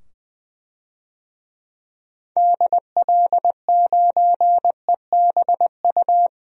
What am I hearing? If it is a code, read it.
Morse code, heard as a keyed tone, about 20 wpm, 700 Hz. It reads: DL9EBU
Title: DXCC-100 Call Signs - 6 Characters - 20wpm